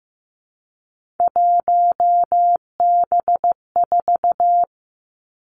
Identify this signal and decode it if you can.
Morse code, heard as a keyed tone, about 15 wpm, 700 Hz. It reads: 1B4
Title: Three Challenging Letter-Number Combinations 15wpm